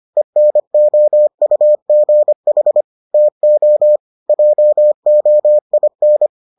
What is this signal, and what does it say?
Morse code, heard as a keyed tone, about 25 wpm, 600 Hz. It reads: ENOUGH TO JOIN